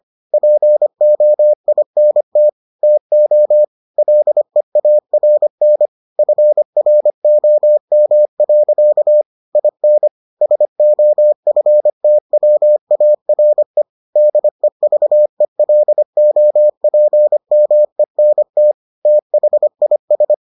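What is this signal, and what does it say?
Morse code, heard as a keyed tone, about 25 wpm, 600 Hz. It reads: POINT TO LEARN FROM. IN SOFTWARE DEVELOPMENT THIS
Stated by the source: Morse edition of the Foundations of Amateur Radio episodes.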